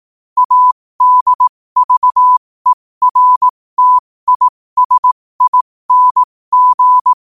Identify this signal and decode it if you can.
Morse code, heard as a keyed tone, about 18 wpm, spaced as for 15 wpm, 1000 Hz. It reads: ADVERTISING
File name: qotdfiles / 15wpm